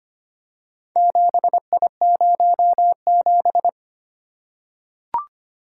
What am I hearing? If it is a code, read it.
Morse code, heard as a keyed tone, about 25 wpm, 700 Hz. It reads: 7I07
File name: practice / 25wpm / Four Letter-Number Combinations - 25wpm